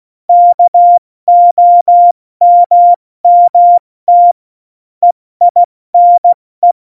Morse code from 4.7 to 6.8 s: EINE